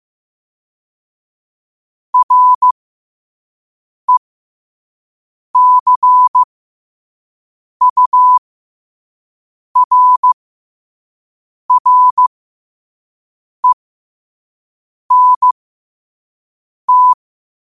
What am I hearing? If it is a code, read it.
Morse code, heard as a keyed tone, about 15 wpm, spaced as for 5 wpm, 1000 Hz. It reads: RECURRENT
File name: qotdfiles / 5wpm